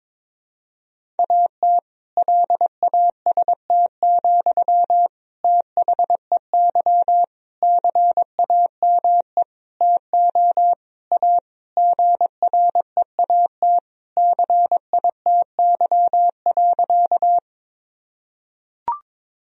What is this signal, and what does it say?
Morse code, heard as a keyed tone, about 22 wpm, 700 Hz. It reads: AT LAST, THEY CAME TO A GREAT CITY.
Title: Sentences from Top 1000 Words - Review Part 1 - 22wpm